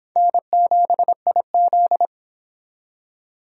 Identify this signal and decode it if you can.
Morse code, heard as a keyed tone, about 23 wpm, 700 Hz. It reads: N7IZ